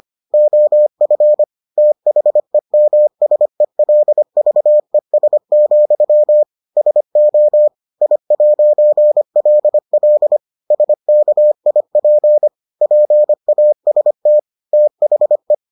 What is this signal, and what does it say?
Morse code, heard as a keyed tone, about 25 wpm, 600 Hz. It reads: OF THEMSELVES, SO I'LL SKIP PAST THE